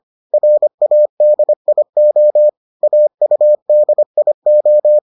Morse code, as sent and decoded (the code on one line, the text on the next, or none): .-. .- -.. .. --- / .- ..- -.. .. ---
RADIO AUDIO